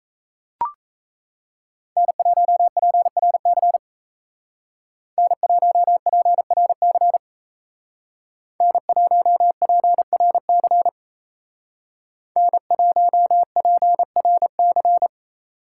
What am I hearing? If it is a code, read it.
Morse code, heard as a keyed tone, about 36 wpm, 700 Hz. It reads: N1PRC N1PRC N1PRC N1PRC